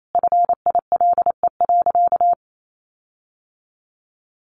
Morse code, heard as a keyed tone, about 28 wpm, 700 Hz. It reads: FILE.
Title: Sentences from Top 1400 Words 28wpm